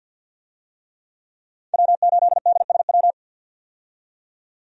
Morse code, heard as a keyed tone, about 52 wpm, 700 Hz. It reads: W8DSW
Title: USA Call Signs - 5 Characters - 50wpm